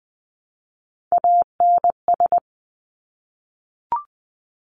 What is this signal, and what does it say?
Morse code, heard as a keyed tone, about 20 wpm, 700 Hz. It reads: ANS
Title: Top Three Letter Combinations in Words 20wpm